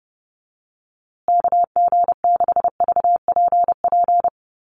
Morse code, heard as a keyed tone, about 29 wpm, 700 Hz. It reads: KG6VPP
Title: USA Call Signs Repeated 3x 30wpm